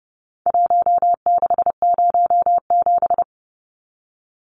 Morse code, heard as a keyed tone, about 30 wpm, 700 Hz. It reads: 1607
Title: Numbers - Contest Style - 4-Digit 30wpm